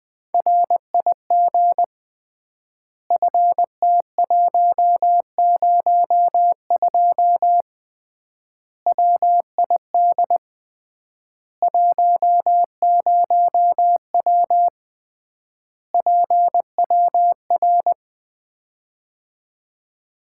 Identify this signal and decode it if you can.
Morse code, heard as a keyed tone, about 20 wpm, 700 Hz. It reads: RIG FT102 WID 10W PWR
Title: Common QSO Phrases - 3x Word-Spacing - 20wpm